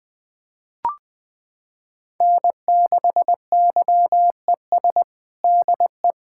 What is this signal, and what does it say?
Morse code, heard as a keyed tone, about 20 wpm, 700 Hz. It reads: N6YES DE
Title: Practice QSO - Exchange 4 - Ending - 20wpm